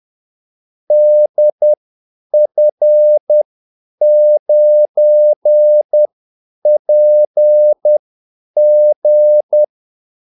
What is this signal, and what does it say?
Morse code, heard as a keyed tone, about 10 wpm, spaced as for 8 wpm, 600 Hz. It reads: DF9PG
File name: Callsigns-10-8wpm